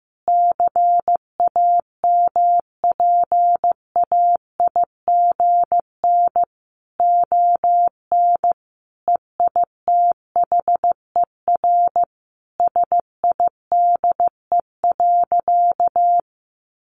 Morse code, heard as a keyed tone, about 15 wpm, 700 Hz. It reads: CAMPAIGN ON EITHER SIDE.
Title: Sentences from Top 1200 Words 15wpm